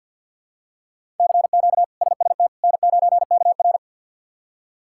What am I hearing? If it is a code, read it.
Morse code, heard as a keyed tone, about 50 wpm, 700 Hz. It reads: CQ SST N9KR
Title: K1USN SST QSO Practice - 50wpm